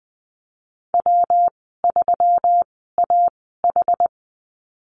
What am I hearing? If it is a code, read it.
Morse code, heard as a keyed tone, about 20 wpm, spaced as for 15 wpm, 700 Hz. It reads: W3AH
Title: 122 Lesson 20g - Number 3 - Call Signs 20wpm